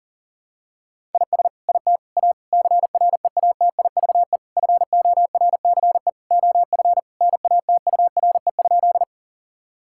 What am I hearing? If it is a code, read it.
Morse code, heard as a keyed tone, about 37 wpm, 700 Hz. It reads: IS IT A CREATIVE FORCE OF NATURE?
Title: Sentences from Top 1000 Words - Review Part 5 - 40wpm